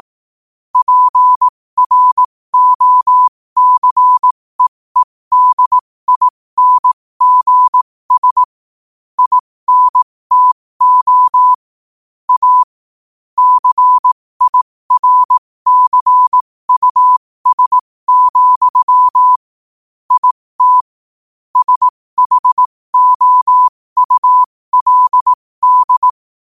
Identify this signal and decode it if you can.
Morse code, heard as a keyed tone, about 18 wpm, spaced as for 15 wpm, 1000 Hz. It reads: PROCEEDINGS INTO A CIRCUS, IT SHOULD
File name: qotdfiles / 15wpm